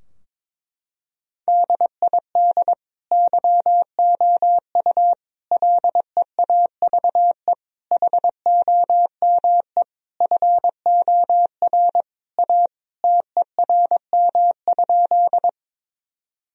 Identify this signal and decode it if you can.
Morse code, heard as a keyed tone, about 21 wpm, 700 Hz. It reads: DID YOU LEAVE HOME FOR A TERM?
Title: Sentences from Top 1000 Words - Review Part 4 - 22wpm